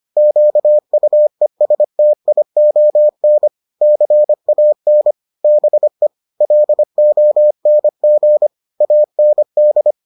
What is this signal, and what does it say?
Morse code, heard as a keyed tone, about 25 wpm, 600 Hz. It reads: QUESTION CAN BE LONG AND